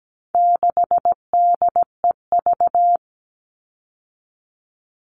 Morse code, heard as a keyed tone, about 17 wpm, 700 Hz. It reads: 6DEV
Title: All Cut Numbers - 4-Digit 17wpm